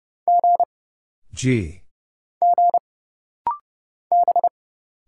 Morse code, heard as a keyed tone, about 30 wpm, 700 Hz. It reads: G G B
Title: Single Letters - Rapid-Fire 30wpm